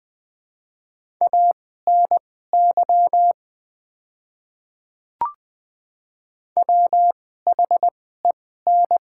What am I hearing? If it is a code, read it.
Morse code, heard as a keyed tone, about 20 wpm, spaced as for 14 wpm, 700 Hz. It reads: ANY WHEN
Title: 2x Character Spacing - Top 200 Words - 20wpm